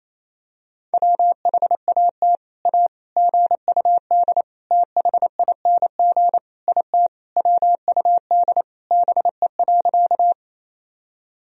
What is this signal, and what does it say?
Morse code, heard as a keyed tone, about 28 wpm, 700 Hz. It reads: WHAT A GUD THING IT WUD BE.
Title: Sentences from Top 200 Words and Abbreviations 28wpm